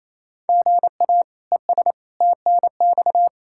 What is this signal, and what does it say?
Morse code, heard as a keyed tone, about 28 wpm, 700 Hz. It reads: GA ES TNX
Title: Practice QSO - Exchange 1 - Beginning - 28wpm